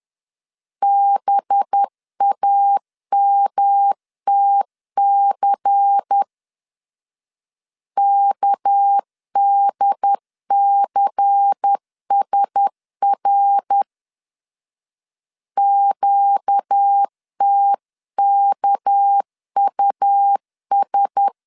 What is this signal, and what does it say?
Morse code, heard as a keyed tone, about 10 wpm, 800 Hz. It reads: BAMTC KDCSR QTKUS
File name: LEZIONI / 6 PKX0 / 6 PK 10wpm 2436